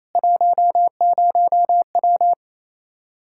Morse code, heard as a keyed tone, about 28 wpm, 700 Hz. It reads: 10W
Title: Practice QSO - Exchange 2 - Info - 28wpm